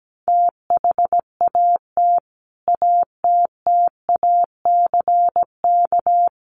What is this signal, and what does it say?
Morse code, heard as a keyed tone, about 17 wpm, 700 Hz. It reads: THAT ATTACK
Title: Sentences from Top 1500 Words 17wpm